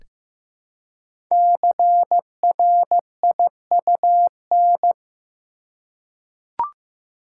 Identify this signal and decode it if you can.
Morse code, heard as a keyed tone, about 15 wpm, 700 Hz. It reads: CRIUN
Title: Code Groups - 5 Letters - 15wpm